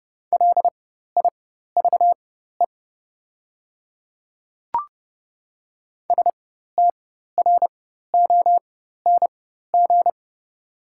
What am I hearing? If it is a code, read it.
Morse code, heard as a keyed tone, about 31 wpm, spaced as for 14 wpm, 700 Hz. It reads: LIVE STRONG
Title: 4x Character Spacing - Top 500 Words - 30wpm